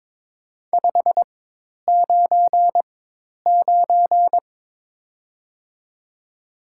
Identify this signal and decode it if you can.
Morse code, heard as a keyed tone, about 21 wpm, 700 Hz. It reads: 5 9 9
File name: Top 90 QSO Elements - 4x Character Spacing - 22wpm